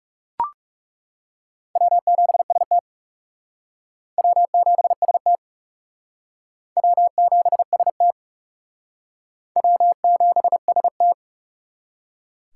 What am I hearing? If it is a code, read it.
Morse code, heard as a keyed tone, about 37 wpm, 700 Hz. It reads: W7ST W7ST W7ST W7ST
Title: Speed Racing - USA Call Signs 30wpm